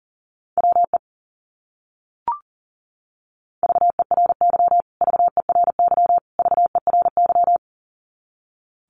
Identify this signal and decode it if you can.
Morse code, heard as a keyed tone, about 40 wpm, 700 Hz. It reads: WE VERY VERY VERY
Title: Top 200 Words Repeated 40wpm